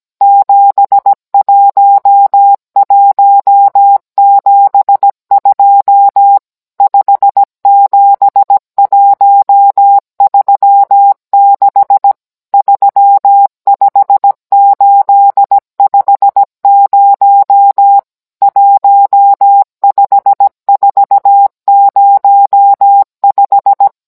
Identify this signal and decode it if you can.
Morse code, heard as a keyed tone, about 17 wpm, 800 Hz. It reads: 71172 57136 35850 15405